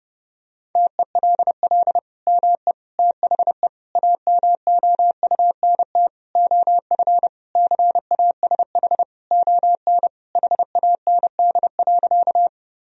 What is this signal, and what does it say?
Morse code, heard as a keyed tone, about 30 wpm, 700 Hz. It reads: TELL ME THE AMOUNT OF CASH ON HAND.
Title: Sentences from Top 1000 Words - Review Part 4 - 30wpm